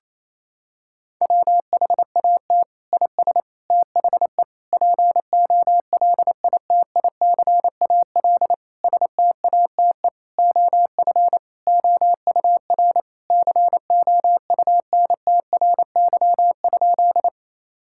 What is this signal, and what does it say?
Morse code, heard as a keyed tone, about 28 wpm, 700 Hz. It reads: WHAT IS THE POLITICAL STATE OF OUR COUNTRY?
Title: Sentences from Top 1000 Words - Review Part 5 - 28wpm